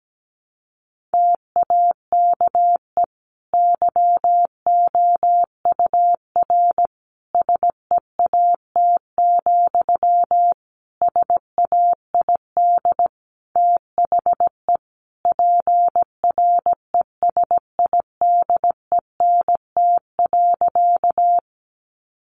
Morse code, heard as a keyed tone, about 17 wpm, 700 Hz. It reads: TAKE YOUR SEAT, SAID THE PRESIDENT.
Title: Sentences from Top 1500 Words 17wpm